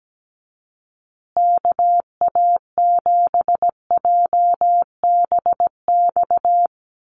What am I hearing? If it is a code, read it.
Morse code, heard as a keyed tone, about 17 wpm, 700 Hz. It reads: KA7JBX
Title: USA Call Signs Repeated 3x 17wpm